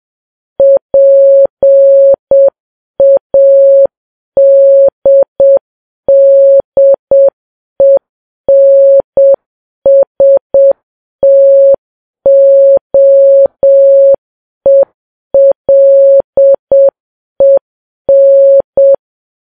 Morse code, heard as a keyed tone, about 7 wpm, 550 Hz. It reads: PADDENSTOELEN
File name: Files Web tls / Morse on9fv / morse7kps